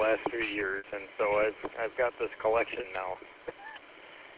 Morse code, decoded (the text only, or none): none